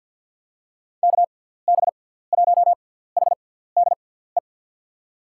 Morse code, heard as a keyed tone, about 53 wpm, spaced as for 17 wpm, 700 Hz. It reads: KB1HDE
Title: USA Call Signs - 6 Characters - 6x Character Spacing - 50wpm